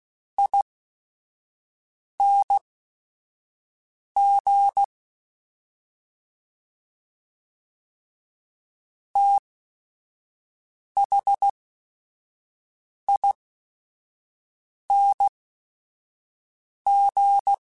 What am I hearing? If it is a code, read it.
Morse code, heard as a keyed tone, about 16 wpm, spaced as for 5 wpm, 800 Hz. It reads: ING THING
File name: Special INGs / Special INGs-mp3s / SlowToFaster ing @5